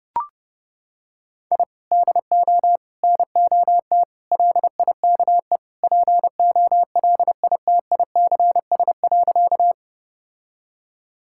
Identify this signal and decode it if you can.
Morse code, heard as a keyed tone, about 32 wpm, 700 Hz. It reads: I DO NOT LIKE POLITICS.
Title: Sentences from Top 1000 Words - Review Part 6 - 30wpm